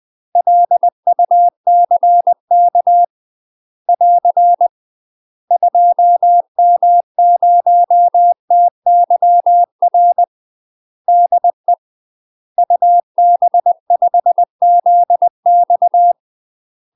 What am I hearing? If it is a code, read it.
Morse code, heard as a keyed tone, about 20 wpm, 700 Hz. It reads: LUCK <AR> 2M0TYR DE UB5ZX
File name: UK MASTER QSOs TRAINING / 20WPM / UK MASTER QSOs 0064